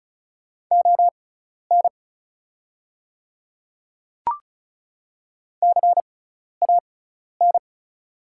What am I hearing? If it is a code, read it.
Morse code, heard as a keyed tone, about 36 wpm, spaced as for 12 wpm, 700 Hz. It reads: ON CAN
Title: 6x Character Spacing - Top 100 Words - 35wpm